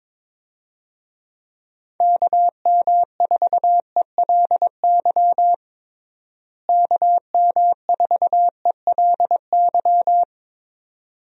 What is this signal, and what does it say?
Morse code, heard as a keyed tone, about 21 wpm, 700 Hz. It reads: KM4ELY KM4ELY
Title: USA Call Signs - 6 Characters - Repeated 2x - 22wpm